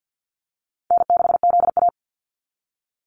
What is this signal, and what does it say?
Morse code, heard as a keyed tone, about 50 wpm, 700 Hz. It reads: N6ZA
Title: NA SOTA Activator Callsigns 50wpm